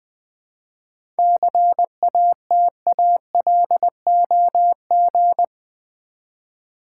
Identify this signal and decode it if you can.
Morse code, heard as a keyed tone, about 20 wpm, 700 Hz. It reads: CATALOG
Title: 7 Letter Words - First 3 Letters Given - 20wpm